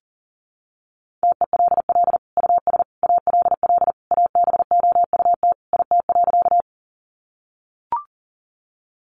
Morse code, heard as a keyed tone, about 40 wpm, 700 Hz. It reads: TELL US ALL ABOUT IT.
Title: Sentences from Top 200 Words 40wpm